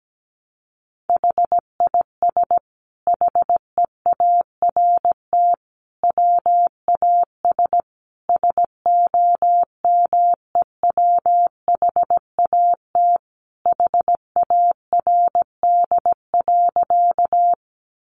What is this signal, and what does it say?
Morse code, heard as a keyed tone, about 17 wpm, 700 Hz. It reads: HIS HEART WAS SOMEWHAT HARD.
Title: Sentences from Top 1300 Words 17wpm